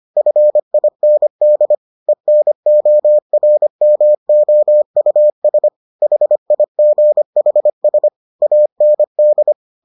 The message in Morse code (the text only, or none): FIND ENORMOUS HIGHS AND